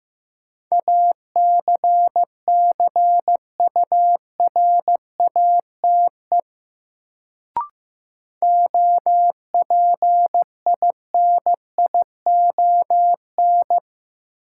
Morse code, heard as a keyed tone, about 15 wpm, 700 Hz. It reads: ACCURATE OPINION
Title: Sentences from Top 1200 Words - New Word Drill Alt - 15wpm